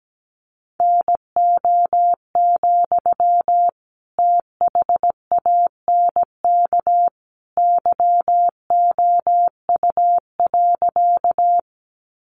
Morse code, heard as a keyed tone, about 17 wpm, 700 Hz. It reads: NO, THANK YOU.
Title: Sentences from Top 600 Words 17wpm